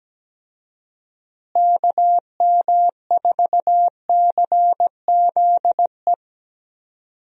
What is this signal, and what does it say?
Morse code, heard as a keyed tone, about 17 wpm, 700 Hz. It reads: KM4CZE
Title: USA Call Signs - 6 Characters - Repeated 3x - 17wpm